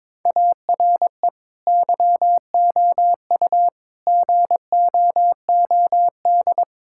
Morse code, heard as a keyed tone, about 22 wpm, 700 Hz. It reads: ARE YOU GOOD
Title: Sentences from Top 1000 Words - Review Part 6 - 22wpm